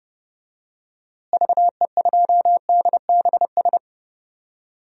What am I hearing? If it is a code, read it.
Morse code, heard as a keyed tone, about 30 wpm, 700 Hz. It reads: VE2DBS
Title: DXCC-100 Call Signs - 6 Characters - Repeated 2x - 30wpm